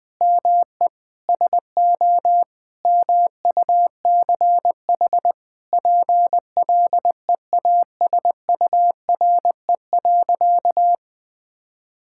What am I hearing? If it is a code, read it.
Morse code, heard as a keyed tone, about 20 wpm, 700 Hz. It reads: ME SO MUCH PLEASURE.
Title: Sentences from Top 1600 Words 20wpm